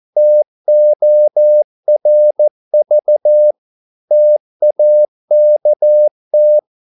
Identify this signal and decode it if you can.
Morse code, heard as a keyed tone, about 14 wpm, 600 Hz. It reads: TORV TAKT
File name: Korte ord DK 14wpm